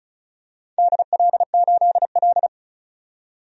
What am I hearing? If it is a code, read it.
Morse code, heard as a keyed tone, about 36 wpm, 700 Hz. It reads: DL8L
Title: DXCC-100 Call Signs - 4 Characters - 35wpm